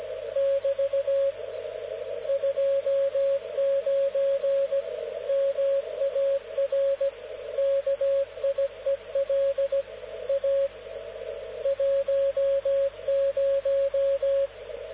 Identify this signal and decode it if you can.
Morse code, heard as a keyed tone, about 17 wpm, 550 Hz, with radio noise. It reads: <BT> 29 MAR KIEL A 10
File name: luxorion / Radio / dk0wcy-spaceweather-10144khz